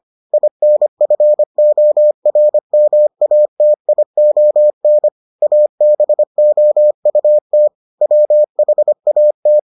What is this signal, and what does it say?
Morse code, heard as a keyed tone, about 25 wpm, 600 Hz. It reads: INFORMATION ABOUT WHAT